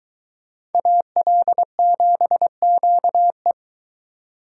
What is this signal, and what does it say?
Morse code, heard as a keyed tone, about 23 wpm, 700 Hz. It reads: AL7QE